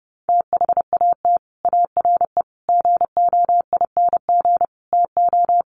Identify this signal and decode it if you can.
Morse code, heard as a keyed tone, about 28 wpm, 700 Hz. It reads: THAT ARE GOING TO
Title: Sets of 5 Words Encore2 30wpm